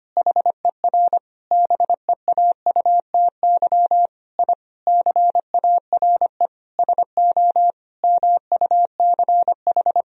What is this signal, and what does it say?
Morse code, heard as a keyed tone, about 25 wpm, 700 Hz. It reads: HER BEAUTY I CARE SO MUCH
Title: Sentences from Top 1000 Words - Review Part 2 - 25wpm